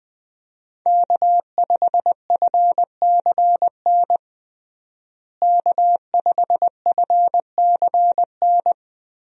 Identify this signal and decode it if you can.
Morse code, heard as a keyed tone, about 21 wpm, 700 Hz. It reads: K5FCN K5FCN
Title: USA Call Signs - 5 Characters - Repeated 3x - 20wpm